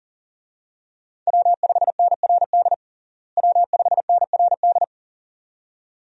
Morse code, heard as a keyed tone, about 37 wpm, 700 Hz. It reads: W5NRD W5NRD
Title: USA Call Signs - 5 Characters - Repeated 2x - 40wpm